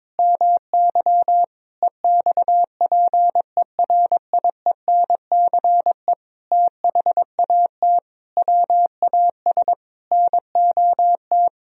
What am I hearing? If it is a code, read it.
Morse code, heard as a keyed tone, about 22 wpm, 700 Hz. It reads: MY EXPERIENCE THAT WAS NOT
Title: Sentences from Top 1000 Words - Review Part 4 - 22wpm